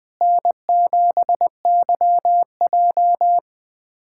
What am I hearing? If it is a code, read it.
Morse code, heard as a keyed tone, about 20 wpm, 700 Hz. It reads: N7YJ
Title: USA Callsign DE Callsign - 4 Characters - 20wpm